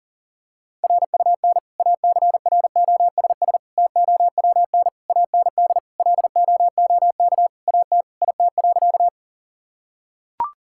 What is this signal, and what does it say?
Morse code, heard as a keyed tone, about 40 wpm, 700 Hz. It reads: RUN ACROSS TOWN AND LOOK AT IT.
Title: Sentences from Top 1000 Words - Review Part 5 - 40wpm